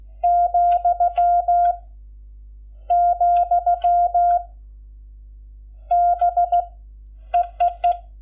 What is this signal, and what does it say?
Morse code, heard as a keyed tone, about 15 wpm, 690 Hz. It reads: , , B EEE
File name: Lecciones / CLASE Nº11